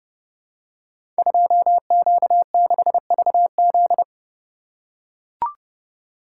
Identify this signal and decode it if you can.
Morse code, heard as a keyed tone, about 30 wpm, 700 Hz. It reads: 2Q6VZ
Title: Code Groups - 5 Letter:Numbers - 30wpm